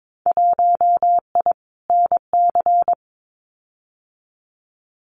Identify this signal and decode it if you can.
Morse code, heard as a keyed tone, about 22 wpm, 700 Hz. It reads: 1I NC
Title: Winter Field Day Exchanges 22wpm